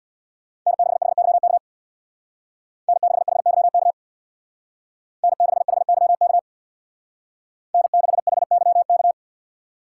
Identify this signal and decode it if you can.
Morse code, heard as a keyed tone, about 61 wpm, 700 Hz. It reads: N6HYK N6HYK N6HYK N6HYK
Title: Speed Racing - USA 1x3 Call Signs 50wpm